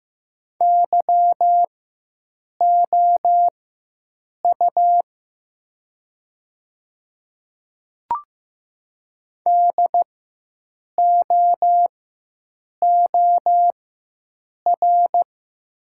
Morse code, heard as a keyed tone, about 15 wpm, spaced as for 7 wpm, 700 Hz. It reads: YOU DOOR
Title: 4x Character Spacing - Top 300 Words - 15wpm